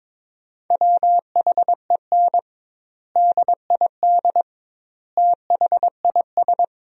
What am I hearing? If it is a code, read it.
Morse code, heard as a keyed tone, about 22 wpm, 700 Hz. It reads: WHEN DID THIS
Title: 2x Word-Spacing - Sentences from Top 400 Words - 22wpm